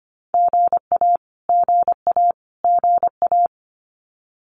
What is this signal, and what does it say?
Morse code, heard as a keyed tone, about 25 wpm, 700 Hz. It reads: GA GA GA
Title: Abbreviations Repeated 3x - 25wpm